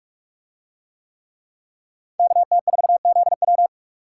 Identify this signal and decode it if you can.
Morse code, heard as a keyed tone, about 45 wpm, 700 Hz. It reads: KT4ZW